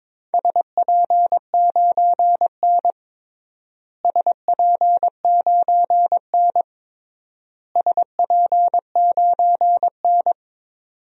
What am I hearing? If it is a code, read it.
Morse code, heard as a keyed tone, about 22 wpm, 700 Hz. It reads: SP9N SP9N SP9N
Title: DXCC-100 Call Signs - 4 Characters - Repeated 3x - 22wpm